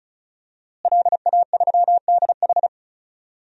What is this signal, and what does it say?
Morse code, heard as a keyed tone, about 36 wpm, 700 Hz. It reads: LA3DH
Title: DXCC-100 Call Signs - 5 Characters - Repeated 2x - 35wpm